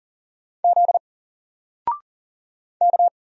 Morse code, heard as a keyed tone, about 40 wpm, 700 Hz. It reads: Z K
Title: Single Letter-Number Alt 40wpm